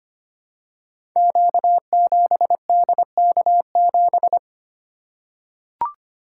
Morse code, heard as a keyed tone, about 25 wpm, 700 Hz. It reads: Q7DK7
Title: Code Groups - 5 Letter:Numbers - 25wpm